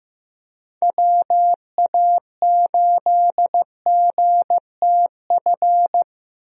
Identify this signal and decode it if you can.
Morse code, heard as a keyed tone, about 16 wpm, 700 Hz. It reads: WA8GTF
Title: USA Call Signs - 6 Characters - Repeated 3x - 15wpm